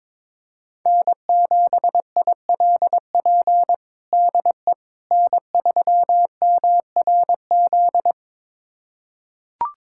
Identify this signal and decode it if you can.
Morse code, heard as a keyed tone, about 21 wpm, 700 Hz. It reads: N7ILP DE N3MRZ
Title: USA Callsign DE Callsign - 5 Characters - 22wpm